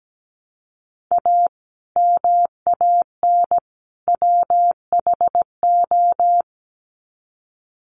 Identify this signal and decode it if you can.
Morse code, heard as a keyed tone, about 17 wpm, 700 Hz. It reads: A MAN WHO
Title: Sets of 3 Words 17wpm